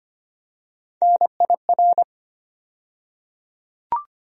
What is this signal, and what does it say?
Morse code, heard as a keyed tone, about 25 wpm, 700 Hz. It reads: NIR
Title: USA Call Sign Postfixes - 3 Letters 25wpm